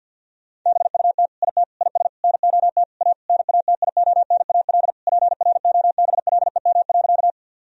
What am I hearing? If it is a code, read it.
Morse code, heard as a keyed tone, about 50 wpm, 700 Hz. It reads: BUT IT IS NOT A NATIONAL PROBLEM.
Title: Sentences from Top 1000 Words - Review Part 4 - 50wpm